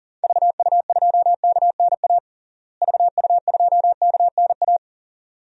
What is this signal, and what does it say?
Morse code, heard as a keyed tone, about 40 wpm, 700 Hz. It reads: VU2KNA VU2KNA
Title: DXCC-100 Call Signs - 6 Characters - Repeated 2x - 40wpm